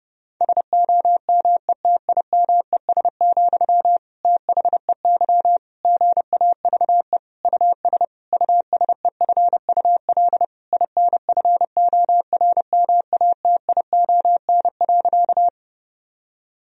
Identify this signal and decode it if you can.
Morse code, heard as a keyed tone, about 30 wpm, 700 Hz. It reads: SOMETIMES, THEY GAVE US USEFUL INFORMATION.
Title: Sentences from Top 1000 Words - Review Part 4 - 30wpm